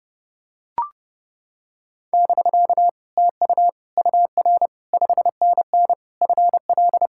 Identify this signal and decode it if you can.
Morse code, heard as a keyed tone, about 29 wpm, 700 Hz. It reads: <BK> TU UR 5NN FL